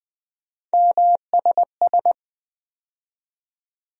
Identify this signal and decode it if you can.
Morse code, heard as a keyed tone, about 20 wpm, 700 Hz. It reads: MSS
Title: USA Call Sign Postfixes - 3 Letters 20wpm